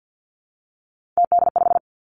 Morse code, heard as a keyed tone, about 50 wpm, 700 Hz. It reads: TD5
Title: Three Challenging Letter-Number Combinations 50wpm